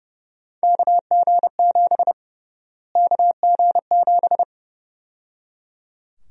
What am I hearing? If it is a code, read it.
Morse code, heard as a keyed tone, about 32 wpm, 700 Hz. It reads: KG7 KG7
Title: USA Call Sign Prefixes - Repeated 2x - 30wpm